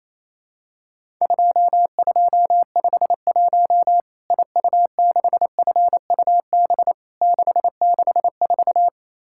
Morse code, heard as a keyed tone, about 27 wpm, 700 Hz. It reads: 2251 IU6FUB 664
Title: Worked All Europe DX Contest - QTC Exchange - 28wpm